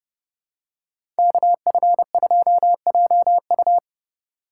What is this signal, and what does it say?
Morse code, heard as a keyed tone, about 30 wpm, 700 Hz. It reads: KF2JU
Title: USA Call Signs - 5 Characters - 6x Character Spacing - 30wpm